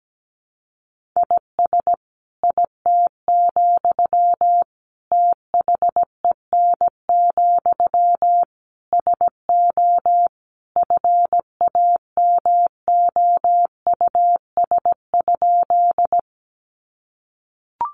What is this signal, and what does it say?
Morse code, heard as a keyed tone, about 17 wpm, 700 Hz. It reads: IS IT, THEN, SO FAMOUS?
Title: Sentences from Top 1500 Words 17wpm